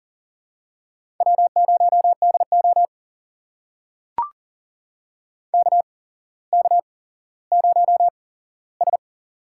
Morse code, heard as a keyed tone, about 40 wpm, spaced as for 17 wpm, 700 Hz. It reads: W0DO KK0S
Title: USA Call Signs - 4 Characters - 8x Character Spacing - 40wpm